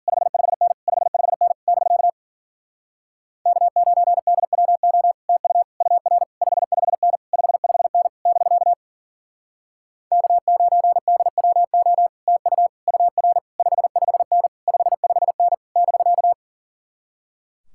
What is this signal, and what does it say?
Morse code, heard as a keyed tone, about 50 wpm, 700 Hz. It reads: 55N 55N <BK> K9DWO TU UR 55N 55N <BK> K9DWO TU UR 55N 55N <BK>